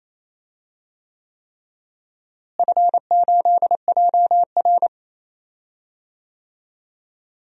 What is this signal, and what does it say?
Morse code, heard as a keyed tone, about 28 wpm, 700 Hz. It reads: F8JR
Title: DXCC-100 Call Signs - 4 Characters - Repeated 3x - 28wpm